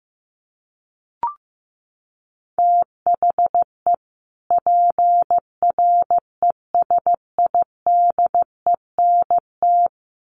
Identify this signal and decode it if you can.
Morse code, heard as a keyed tone, about 15 wpm, 700 Hz. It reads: THE PRESIDENT
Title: Sets of 2 Words 15wpm